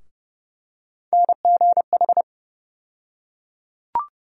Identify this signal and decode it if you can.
Morse code, heard as a keyed tone, about 32 wpm, 700 Hz. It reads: NGH
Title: USA Call Sign Postfixes - 3 Letters 30wpm